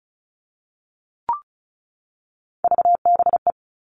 Morse code, heard as a keyed tone, about 34 wpm, 700 Hz. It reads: VBE
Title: All Cut Numbers - 3-Digit 35wpm